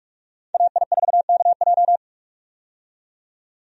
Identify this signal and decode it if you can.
Morse code, heard as a keyed tone, about 42 wpm, 700 Hz. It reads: AI4KJ
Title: USA Call Signs - 5 Characters - 45wpm